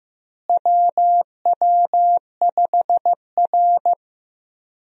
Morse code, heard as a keyed tone, about 15 wpm, 700 Hz. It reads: WW5R
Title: USA Call Signs - 4 Characters - Repeated 2x - 15wpm